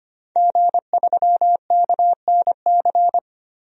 Morse code, heard as a keyed tone, about 25 wpm, 700 Hz. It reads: G3KNC
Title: DXCC-100 Call Signs - 5 Characters - Repeated 3x - 25wpm